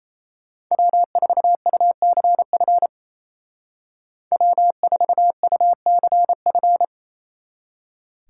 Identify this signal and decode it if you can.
Morse code, heard as a keyed tone, about 34 wpm, 700 Hz. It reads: W4UCF W4UCF
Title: Speed Racing - USA 1x3 Call Signs 28wpm